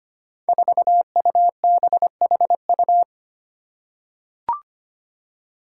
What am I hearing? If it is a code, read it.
Morse code, heard as a keyed tone, about 25 wpm, 700 Hz. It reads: 4UBHU
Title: Five Challenging Letter-Number Combinations 25wpm